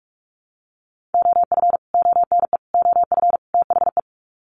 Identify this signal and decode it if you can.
Morse code, heard as a keyed tone, about 44 wpm, 700 Hz. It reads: OF ONE OF THE
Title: Sets of 4 Words 45wpm